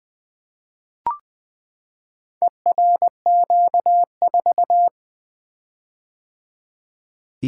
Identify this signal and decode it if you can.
Morse code, heard as a keyed tone, about 20 wpm, 700 Hz. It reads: ERQ4
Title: Four Letter-Number Combinations - 20wpm